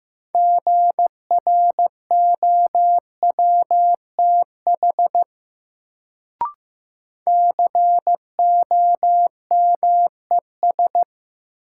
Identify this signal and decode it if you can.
Morse code, heard as a keyed tone, about 15 wpm, 700 Hz. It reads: GROWTH COMES
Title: Sentences from Top 1000 Words - Review Part 5 - New Word Drill - 15wpm